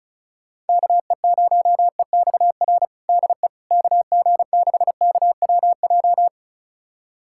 Morse code, heard as a keyed tone, about 36 wpm, 700 Hz. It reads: KE0EXR DE KG6KWJ
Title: USA Callsign DE Callsign - 6 Characters - 35wpm